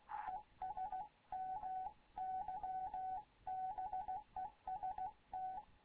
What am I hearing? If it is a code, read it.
Morse code, heard as a keyed tone, about 16 wpm, 750 Hz, with radio noise. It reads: ES MY BEST
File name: Morze-HA2MN / hangp / 2 11 GYAKORLAT